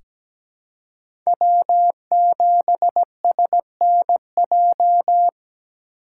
Morse code, heard as a keyed tone, about 17 wpm, 700 Hz. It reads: W7SNJ
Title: USA Call Signs - 3x Character Spacing - 17wpm